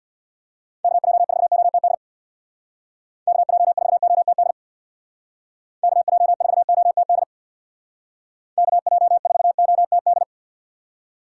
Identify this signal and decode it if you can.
Morse code, heard as a keyed tone, about 65 wpm, 700 Hz. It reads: KJ4OTD KJ4OTD KJ4OTD KJ4OTD